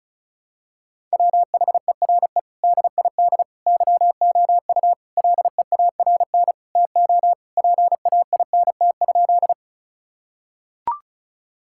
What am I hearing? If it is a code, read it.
Morse code, heard as a keyed tone, about 34 wpm, 700 Hz. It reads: WHERE DID YOU LEARN TO PAINT?
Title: Sentences from Top 1000 Words - Review Part 6 - 35wpm